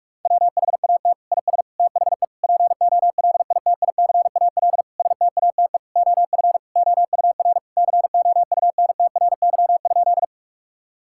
Morse code, heard as a keyed tone, about 46 wpm, 700 Hz. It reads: WHAT IS THE POLITICAL STATE OF OUR COUNTRY?
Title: Sentences from Top 1000 Words - Review Part 5 - 45wpm